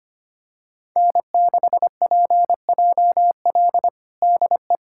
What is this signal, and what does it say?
Morse code, heard as a keyed tone, about 25 wpm, 700 Hz. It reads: N6PJL DE
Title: USA Callsign DE Callsign - 5 Characters - 25wpm